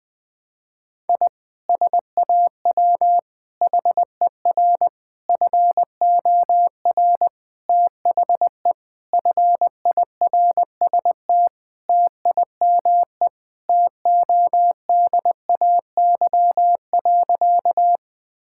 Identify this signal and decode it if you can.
Morse code, heard as a keyed tone, about 20 wpm, 700 Hz. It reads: I SAW HER FOR THE FIRST TIME TODAY.
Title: Sentences with 4-Word Phrases - Spoken Beginning - 20wpm